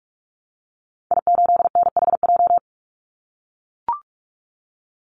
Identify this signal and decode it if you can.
Morse code, heard as a keyed tone, about 44 wpm, 700 Hz. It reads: I8NHJ
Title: Call Signs - 45wpm